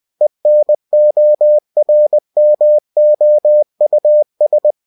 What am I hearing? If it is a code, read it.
Morse code, heard as a keyed tone, about 20 wpm, 600 Hz. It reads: ENORMOUS
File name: cw-books / winnie-the-pooh / winnie-20wpm-page-0043